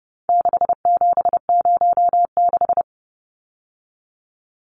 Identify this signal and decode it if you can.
Morse code, heard as a keyed tone, about 30 wpm, 700 Hz. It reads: 6706
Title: Numbers - 4-Digit 30wpm